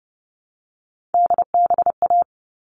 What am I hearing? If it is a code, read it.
Morse code, heard as a keyed tone, about 30 wpm, 700 Hz. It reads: DBA
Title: All Cut Numbers - 3-Digit 30wpm